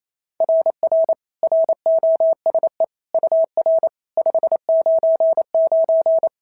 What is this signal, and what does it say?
Morse code, heard as a keyed tone, about 27 wpm, 690 Hz. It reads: RR ROSE UR 599
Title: SOTA QSO Practice 28wpm